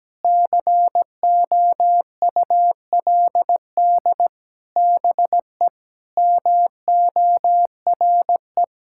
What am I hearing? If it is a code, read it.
Morse code, heard as a keyed tone, about 17 wpm, 700 Hz. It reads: COULD BE MORE
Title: Sentences from Top 1000 Words - Review Part 6 - 17wpm